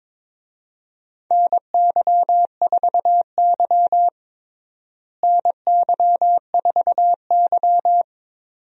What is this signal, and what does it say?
Morse code, heard as a keyed tone, about 22 wpm, 700 Hz. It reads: NY4Y NY4Y
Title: USA Call Signs - 4 Characters - Repeated 2x - 22wpm